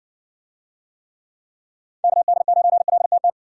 Morse code, heard as a keyed tone, about 61 wpm, 700 Hz. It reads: KD9LTT